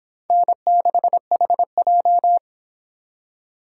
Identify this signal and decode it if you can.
Morse code, heard as a keyed tone, about 25 wpm, 700 Hz. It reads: N6HJ